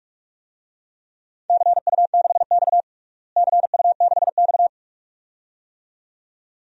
Morse code, heard as a keyed tone, about 46 wpm, 700 Hz. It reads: CU6X CU6X
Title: DXCC-100 Call Signs - 4 Characters - Repeated 2x - 45wpm